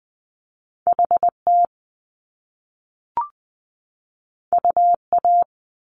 Morse code, heard as a keyed tone, about 20 wpm, 700 Hz. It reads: H T U A
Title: Top Two Letter Combinations in Words 20wpm